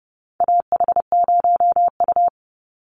Morse code, heard as a keyed tone, about 30 wpm, 700 Hz. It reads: AH0U
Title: Call Signs - 30wpm